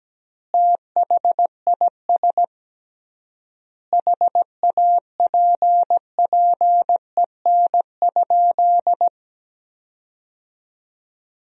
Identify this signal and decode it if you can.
Morse code, heard as a keyed tone, about 17 wpm, 700 Hz. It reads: THIS HAPPEN?
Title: 3x Word-Spacing - Sentences from Top 400 Words - 17wpm